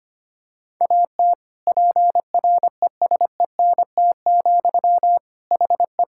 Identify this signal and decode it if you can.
Morse code, heard as a keyed tone, about 25 wpm, 700 Hz. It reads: AT PRESENT, HE
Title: Sentences from Top 1000 Words - Review Part 5 - 25wpm